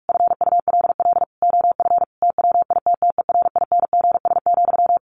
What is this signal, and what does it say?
Morse code, heard as a keyed tone, about 44 wpm, 700 Hz. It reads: FULL OF TWITTERINGS,
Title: A Sound Of Thunder 45wpm